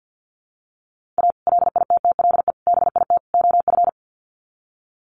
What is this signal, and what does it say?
Morse code, heard as a keyed tone, about 50 wpm, 700 Hz. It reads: A LITTLE BIT OF
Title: Sets of 4 Words 50wpm